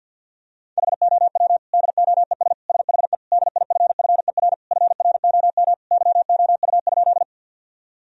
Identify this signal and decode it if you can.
Morse code, heard as a keyed tone, about 52 wpm, 700 Hz. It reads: HOW DOES SHE DIFFER FROM YOU?
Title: Sentences from Top 1000 Words - Review Part 1 - 50wpm